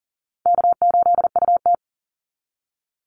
Morse code, heard as a keyed tone, about 40 wpm, 700 Hz. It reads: K8UT
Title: Easy Call Signs Repeated 2x - 40wpm